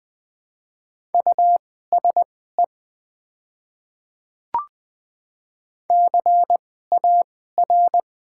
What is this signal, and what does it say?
Morse code, heard as a keyed tone, about 20 wpm, spaced as for 14 wpm, 700 Hz. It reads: USE CAR
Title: 2x Character Spacing - Top 300 Words - 20wpm